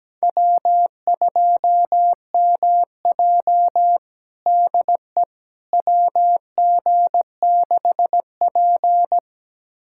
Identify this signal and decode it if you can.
Morse code, heard as a keyed tone, about 17 wpm, 700 Hz. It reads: W2MJ DE WG6P
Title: USA Callsign DE Callsign - 4 Characters - 17wpm